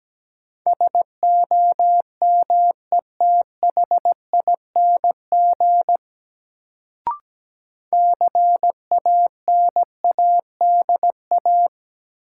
Morse code, heard as a keyed tone, about 17 wpm, 700 Hz. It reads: SOMETHING CANADA
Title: Sets of 5 Words - New Word Drill - 17wpm